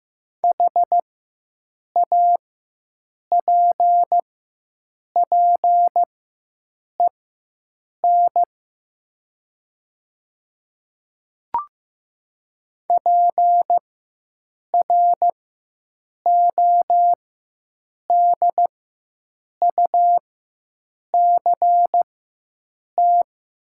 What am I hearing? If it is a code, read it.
Morse code, heard as a keyed tone, about 15 wpm, spaced as for 7 wpm, 690 Hz. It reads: HAPPEN PRODUCT
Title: 4x Character Spacing - Top 500 Words - 15wpm